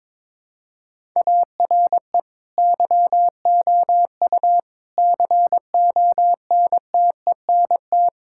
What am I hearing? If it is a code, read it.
Morse code, heard as a keyed tone, about 22 wpm, 700 Hz. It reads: ARE YOU CONTENT
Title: Sentences from Top 1000 Words - Review Part 6 - 22wpm